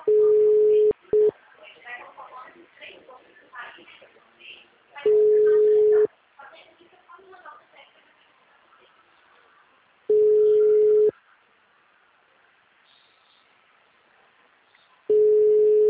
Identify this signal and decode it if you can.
Ringback tone (Europe, 425 Hz), with speech.
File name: out- -2009- - - .3708